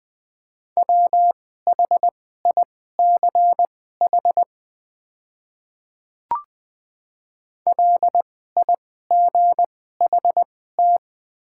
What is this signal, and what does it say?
Morse code, heard as a keyed tone, about 20 wpm, spaced as for 14 wpm, 700 Hz. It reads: WHICH LIGHT
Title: 2x Character Spacing - Top 200 Words - 20wpm